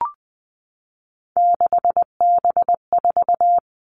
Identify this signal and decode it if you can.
Morse code, heard as a keyed tone, about 20 wpm, 700 Hz. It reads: 6B4
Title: All Cut Numbers - 3-Digit 20wpm